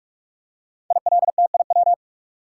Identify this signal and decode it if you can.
Morse code, heard as a keyed tone, about 44 wpm, 700 Hz. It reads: ILTIW
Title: Code Groups - 5 Letters - 45wpm